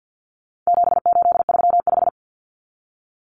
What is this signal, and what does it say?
Morse code, heard as a keyed tone, about 50 wpm, 700 Hz. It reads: 7835
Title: Numbers - 4-Digit 50wpm